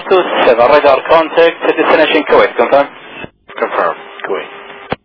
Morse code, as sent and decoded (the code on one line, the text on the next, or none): none